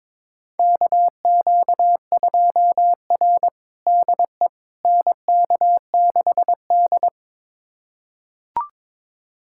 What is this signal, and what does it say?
Morse code, heard as a keyed tone, about 22 wpm, 700 Hz. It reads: KQ2R DE NK6D
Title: USA Callsign DE Callsign - 4 Characters - 22wpm